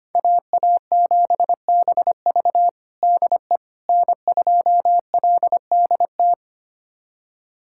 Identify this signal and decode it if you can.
Morse code, heard as a keyed tone, about 25 wpm, 700 Hz. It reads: AA7BV DE N2LDT
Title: USA Callsign DE Callsign - 5 Characters - 25wpm